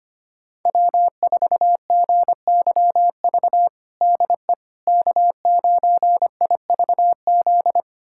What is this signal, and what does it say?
Morse code, heard as a keyed tone, about 25 wpm, 700 Hz. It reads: W4GYV DE K9IVZ
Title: USA Callsign DE Callsign - 5 Characters - 25wpm